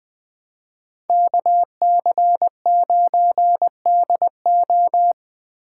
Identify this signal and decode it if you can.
Morse code, heard as a keyed tone, about 21 wpm, 700 Hz. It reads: KC9DO
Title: Speed Racing - USA 2x2 Call Signs 17wpm